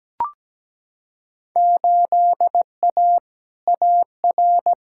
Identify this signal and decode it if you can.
Morse code, heard as a keyed tone, about 17 wpm, 700 Hz. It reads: 8A AR
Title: Field Day Exchanges Encore - 17wpm